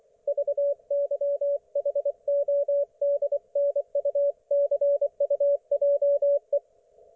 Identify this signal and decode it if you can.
Morse code, heard as a keyed tone, about 24 wpm, 570 Hz. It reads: VYHODNUCUJE
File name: Archiv / CUC224 / ok1fhi cul224